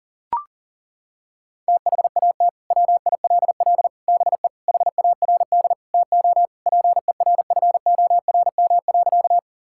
Morse code, heard as a keyed tone, about 37 wpm, 700 Hz. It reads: THAT WILL BE HARD TO PERFORM.
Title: Sentences from Top 1000 Words - Review Part 5 - 40wpm